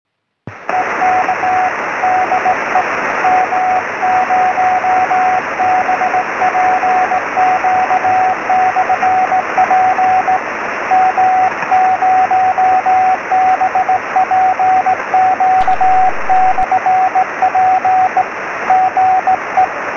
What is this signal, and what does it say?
Morse code, heard as a keyed tone, about 17 wpm, 750 Hz, with radio noise. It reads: EK DE M0BPQ/P M0BPQ/P GE
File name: sm7lcb / m0bpq-p 23cm